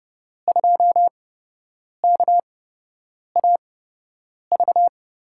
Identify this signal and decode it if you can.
Morse code, heard as a keyed tone, about 30 wpm, spaced as for 8 wpm, 700 Hz. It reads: 2KAV
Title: USA Call Signs - 8x Character Spacing - 30wpm